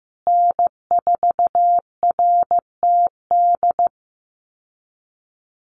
Morse code, heard as a keyed tone, about 15 wpm, 700 Hz. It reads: N4RTD
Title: USA Call Signs Repeated 3x 15wpm